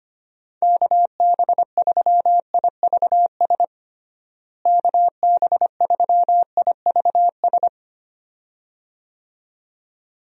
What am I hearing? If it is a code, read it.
Morse code, heard as a keyed tone, about 25 wpm, 700 Hz. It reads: KB3IVS KB3IVS
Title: USA Call Signs - 6 Characters - Repeated 2x - 25wpm